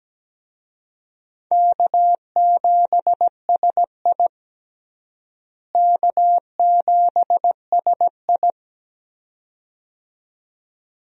Morse code, heard as a keyed tone, about 17 wpm, 700 Hz. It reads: K7SI K7SI
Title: USA Call Signs - 4 Characters - Repeated 2x - 17wpm